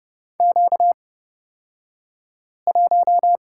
Morse code, heard as a keyed tone, about 30 wpm, spaced as for 5 wpm, 700 Hz. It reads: Q1
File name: Lesson 38d - Letter q - Two Characters 30wpm